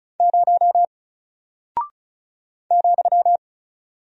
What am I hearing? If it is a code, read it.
Morse code, heard as a keyed tone, about 35 wpm, 700 Hz. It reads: 0 ,
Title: Single Character (Letter:Number:Punctuation:Prosign) Alt 35wpm